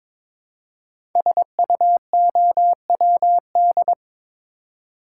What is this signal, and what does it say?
Morse code, heard as a keyed tone, about 22 wpm, 700 Hz. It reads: SUOWD
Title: Code Groups - 5 Letters - 22wpm